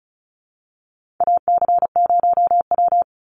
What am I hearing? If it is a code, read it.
Morse code, heard as a keyed tone, about 34 wpm, 700 Hz. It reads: AC0W
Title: Call Signs Repeated 2x - 35wpm